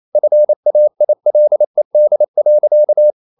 Morse code, heard as a keyed tone, about 28 wpm, 600 Hz. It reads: FAILED.
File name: CW / Practice / EnglishQuotations0228